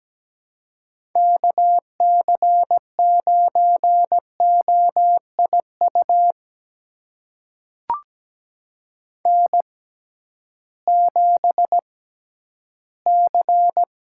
Morse code, heard as a keyed tone, about 17 wpm, spaced as for 11 wpm, 700 Hz. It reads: KC9OIU N7C
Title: USA Call Signs - 6x Character Spacing - 17wpm